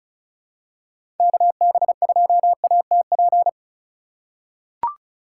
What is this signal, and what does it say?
Morse code, heard as a keyed tone, about 36 wpm, 700 Hz. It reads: KB2ATP
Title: USA Call Signs - 4x Character Spacing - 35wpm